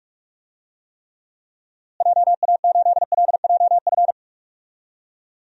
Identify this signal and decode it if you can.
Morse code, heard as a keyed tone, about 46 wpm, 700 Hz. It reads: JA8LJF
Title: DXCC-100 Call Signs - 6 Characters - Repeated 3x - 45wpm